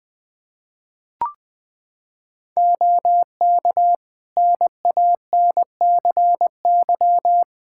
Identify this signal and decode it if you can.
Morse code, heard as a keyed tone, about 21 wpm, 700 Hz. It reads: OK NANCY
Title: Practice QSO - Exchange 2 - Info - 20wpm